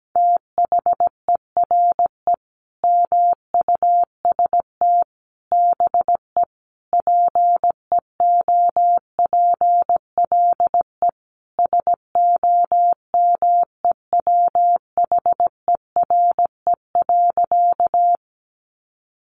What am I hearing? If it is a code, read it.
Morse code, heard as a keyed tone, about 17 wpm, 700 Hz. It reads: THERE MUST BE PEOPLE SOMEWHERE.
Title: Sentences from Top 1400 Words 17wpm